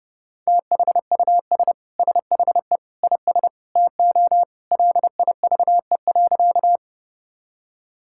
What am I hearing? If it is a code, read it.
Morse code, heard as a keyed tone, about 30 wpm, 700 Hz. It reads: THUS SHE IS TO LIVE.
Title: Sentences from Top 1000 Words - Review Part 5 - 30wpm